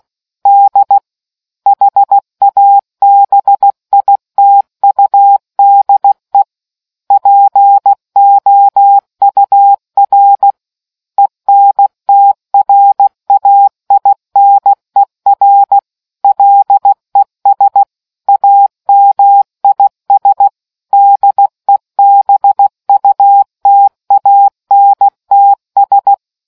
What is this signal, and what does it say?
Morse code, heard as a keyed tone, about 16 wpm, 800 Hz. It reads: D HABITUDE POUR ENTRAINER LES AMIS DEBUTANTS